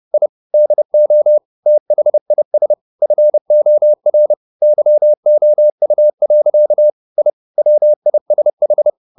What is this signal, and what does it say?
Morse code, heard as a keyed tone, about 30 wpm, 600 Hz. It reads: I DO THIS FOR YOU. I WISH
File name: the old man and the sea / the old man and the sea 1 30wpm